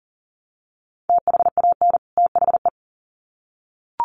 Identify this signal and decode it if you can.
Morse code, heard as a keyed tone, about 40 wpm, 700 Hz. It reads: THAN THE
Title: Sets of 2 Words 40wpm